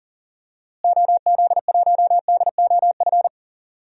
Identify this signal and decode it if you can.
Morse code, heard as a keyed tone, about 40 wpm, 700 Hz. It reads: OZ1DOF
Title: DXCC-100 Call Signs - 6 Characters - Repeated 2x - 40wpm